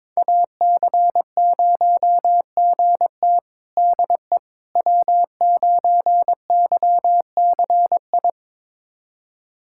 Morse code, heard as a keyed tone, about 21 wpm, 700 Hz. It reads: AC0GT DE W9YCI
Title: USA Callsign DE Callsign - 5 Characters - 22wpm